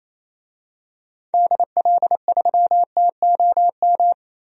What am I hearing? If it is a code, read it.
Morse code, heard as a keyed tone, about 28 wpm, 700 Hz. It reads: DL3TOM
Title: DXCC-100 Call Signs - 6 Characters - 28wpm